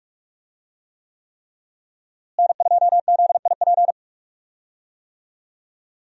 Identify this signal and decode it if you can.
Morse code, heard as a keyed tone, about 42 wpm, 700 Hz. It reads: N2ZIP
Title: USA Call Signs - 5 Characters - Repeated 3x - 45wpm